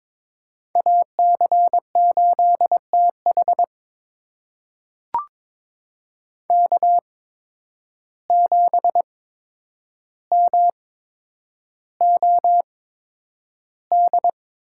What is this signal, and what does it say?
Morse code, heard as a keyed tone, about 22 wpm, spaced as for 9 wpm, 700 Hz. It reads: AC8TH K7MOD
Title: USA Call Signs - 5 Characters - 8x Character Spacing - 22wpm